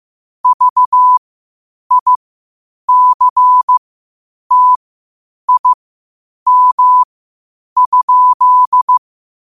Morse code, heard as a keyed tone, about 15 wpm, spaced as for 8 wpm, 1000 Hz. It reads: VICTIM?
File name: qotdfiles / 8wpm